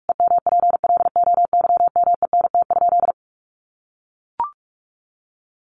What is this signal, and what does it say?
Morse code, heard as a keyed tone, about 44 wpm, 700 Hz. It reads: EMPLOYMENT?
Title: Sentences from Top 1500 Words 45wpm